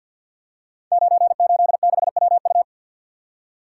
Morse code, heard as a keyed tone, about 51 wpm, 700 Hz. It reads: 986WU
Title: Code Groups - 5 Letter:Numbers - 50wpm